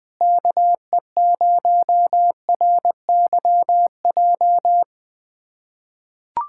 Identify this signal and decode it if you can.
Morse code, heard as a keyed tone, about 20 wpm, 700 Hz. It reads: KE0RYJ
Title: USA Callsign DE Callsign - 6 Characters - 20wpm